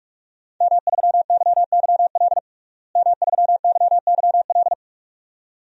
Morse code, heard as a keyed tone, about 44 wpm, 700 Hz. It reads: M3YYL M3YYL
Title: DXCC-100 Call Signs - 5 Characters - Repeated 2x - 45wpm